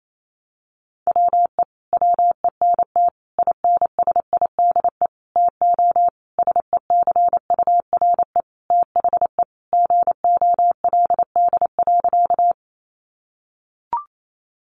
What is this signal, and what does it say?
Morse code, heard as a keyed tone, about 28 wpm, 700 Hz. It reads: WE WENT INSIDE TO SECURE THE GOLD.
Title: Sentences from Top 1200 Words 28wpm